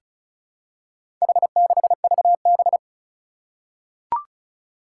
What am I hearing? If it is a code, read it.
Morse code, heard as a keyed tone, about 36 wpm, 700 Hz. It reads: H6VB
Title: Four Challenging Letter-Number Combinations 35wpm